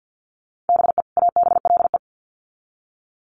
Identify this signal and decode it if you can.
Morse code, heard as a keyed tone, about 50 wpm, 700 Hz. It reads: BE ABLE
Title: Sets of 2 Words 50wpm